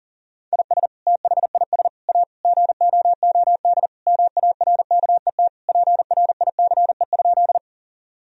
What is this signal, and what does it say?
Morse code, heard as a keyed tone, about 40 wpm, 700 Hz. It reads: IS THIS A GOOD MARKET PRICE?
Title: Sentences from Top 1000 Words - Review Part 3 - 40wpm